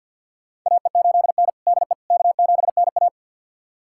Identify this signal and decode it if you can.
Morse code, heard as a keyed tone, about 53 wpm, 700 Hz. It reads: AE8N DE K7NA
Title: USA Callsign DE Callsign - 4 Characters - 50wpm